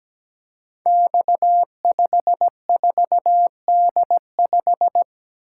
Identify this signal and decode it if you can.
Morse code, heard as a keyed tone, about 17 wpm, 700 Hz. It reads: X54D5
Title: Code Groups - 5 Letter:Numbers - 17wpm